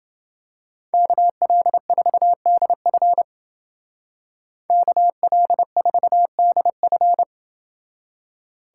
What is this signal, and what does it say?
Morse code, heard as a keyed tone, about 28 wpm, 700 Hz. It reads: KL4DF KL4DF